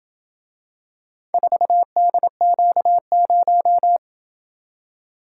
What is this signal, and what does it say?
Morse code, heard as a keyed tone, about 26 wpm, 700 Hz. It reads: 4DQ0